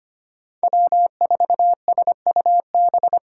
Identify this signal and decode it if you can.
Morse code, heard as a keyed tone, about 25 wpm, 700 Hz. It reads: W4SUB
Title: USA Call Signs - 5 Characters - 3x Character Spacing - 25wpm